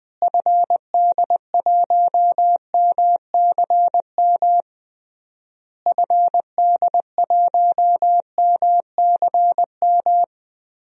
Morse code, heard as a keyed tone, about 20 wpm, 700 Hz. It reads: FD1MCM FD1MCM
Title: DXCC-100 Call Signs - 6 Characters - Repeated 3x - 20wpm